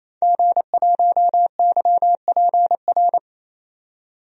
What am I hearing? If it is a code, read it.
Morse code, heard as a keyed tone, about 28 wpm, 700 Hz. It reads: G1YPR
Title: DXCC-100 Call Signs - 5 Characters - 28wpm